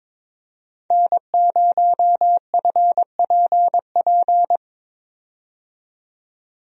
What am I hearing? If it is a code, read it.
Morse code, heard as a keyed tone, about 22 wpm, 700 Hz. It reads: N0FPP
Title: USA Call Signs - 5 Characters - 22wpm